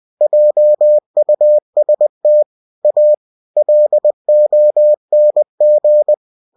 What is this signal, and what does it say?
Morse code, heard as a keyed tone, about 20 wpm, 600 Hz. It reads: JUST A LONG